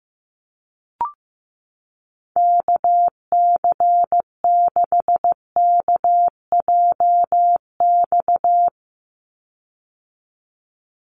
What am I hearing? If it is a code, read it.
Morse code, heard as a keyed tone, about 15 wpm, 700 Hz. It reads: KC6KJX
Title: USA Call Signs 15wpm